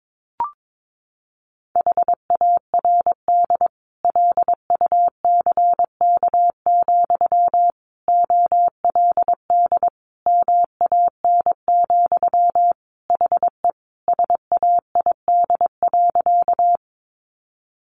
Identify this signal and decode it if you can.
Morse code, heard as a keyed tone, about 22 wpm, 700 Hz. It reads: HARD LUCK, OLD MAN, HE SAID.
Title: Sentences from Top 1500 Words 22wpm